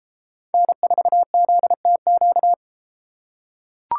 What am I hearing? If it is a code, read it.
Morse code, heard as a keyed tone, about 31 wpm, 700 Hz. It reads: N4ZTQ